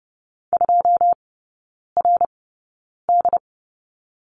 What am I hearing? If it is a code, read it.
Morse code, heard as a keyed tone, about 30 wpm, spaced as for 9 wpm, 700 Hz. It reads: 2RD
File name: 081 Lesson 14e - Number 2 - Three Characters 30wpm